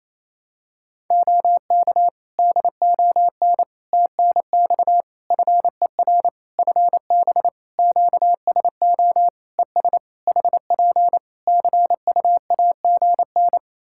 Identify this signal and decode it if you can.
Morse code, heard as a keyed tone, about 27 wpm, 700 Hz. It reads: OK DON TNX FER FB QSO ES HP CUAGN
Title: Practice QSO - Exchange 4 - Ending - 28wpm